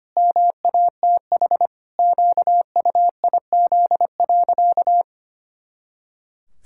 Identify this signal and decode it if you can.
Morse code, heard as a keyed tone, about 25 wpm, 700 Hz. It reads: MATH QUIZ.
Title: Pangram Challenge v2 - 25wpm